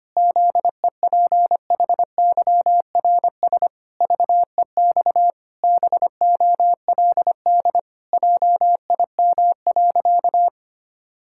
Morse code, heard as a keyed tone, about 25 wpm, 700 Hz. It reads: ZEPHYRS VEX BOLD JIM.
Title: Pangram Challenge v2 - 25wpm